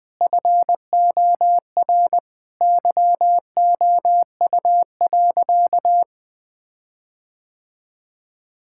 Morse code, heard as a keyed tone, about 20 wpm, 700 Hz. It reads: FOR YOU.
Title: Sentences with 7-Word Phrases - 20wpm